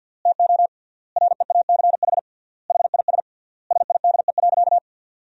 Morse code, heard as a keyed tone, about 52 wpm, 700 Hz. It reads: TO REACH HIS SIDE.
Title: 3x Word-Spacing - Sentences from Top 400 Words - 50wpm